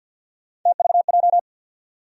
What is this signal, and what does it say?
Morse code, heard as a keyed tone, about 53 wpm, 700 Hz. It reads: TVJ
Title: USA Call Sign Postfixes - 3 Letters 50wpm